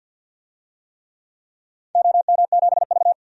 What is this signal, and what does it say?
Morse code, heard as a keyed tone, about 52 wpm, 700 Hz. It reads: OM7V
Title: DXCC-100 Call Signs - 4 Characters - 50wpm